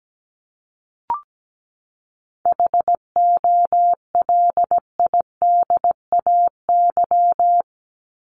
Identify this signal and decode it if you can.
Morse code, heard as a keyed tone, about 17 wpm, 700 Hz. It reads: HOLIDAY
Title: Top 1501-1750 Words 17wpm